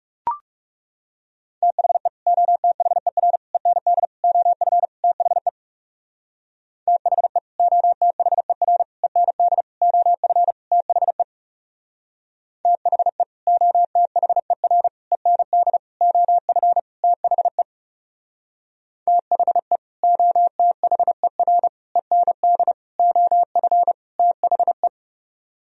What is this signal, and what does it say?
Morse code, heard as a keyed tone, about 37 wpm, 700 Hz. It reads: THE OTHER END OF THE THE OTHER END OF THE THE OTHER END OF THE THE OTHER END OF THE
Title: Speed-Racing - Sets of 5 Words 30wpm